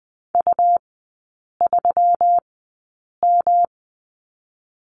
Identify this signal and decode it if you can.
Morse code, heard as a keyed tone, about 20 wpm, spaced as for 8 wpm, 700 Hz. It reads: U3M
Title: 152 Lesson 25e - Letter p - Three Characters 20wpm